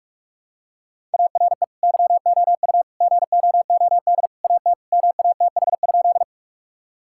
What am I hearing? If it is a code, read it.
Morse code, heard as a keyed tone, about 42 wpm, 700 Hz. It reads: ARE YOU GOOD AT MATH?
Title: Sentences from Top 1000 Words - Review Part 6 - 45wpm